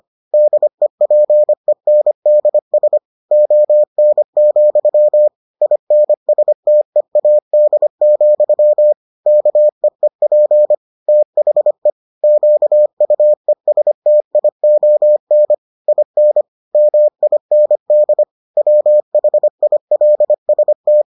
Morse code, heard as a keyed tone, about 25 wpm, 600 Hz. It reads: DEPENDS ON, INSTEAD, KEEP THE QUESTION IN MIND WHILST